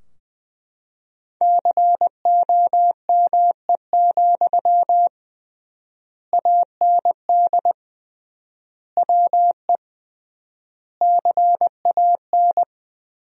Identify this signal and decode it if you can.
Morse code, heard as a keyed tone, about 20 wpm, 700 Hz. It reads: COME, AND WE CAN
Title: 3x Word-Spacing - Sentences from Top 500 Words - 20wpm